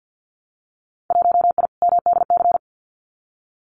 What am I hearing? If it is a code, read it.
Morse code, heard as a keyed tone, about 50 wpm, 700 Hz. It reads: 1I MDC
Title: Winter Field Day Exchanges 50wpm